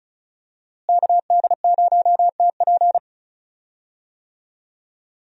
Morse code, heard as a keyed tone, about 36 wpm, 700 Hz. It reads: KD0TP
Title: USA Call Signs - 5 Characters - Repeated 3x - 35wpm